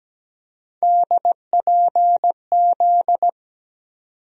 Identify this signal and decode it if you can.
Morse code, heard as a keyed tone, about 17 wpm, 700 Hz. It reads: DPZ
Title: USA Call Sign Postfixes - 3 Letters 17wpm